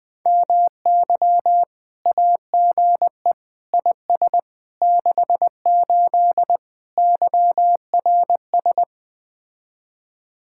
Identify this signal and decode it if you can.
Morse code, heard as a keyed tone, about 20 wpm, 700 Hz. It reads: MY AGE IS 68 YRS
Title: Speed-Racing Common QSO Phrases - 17wpm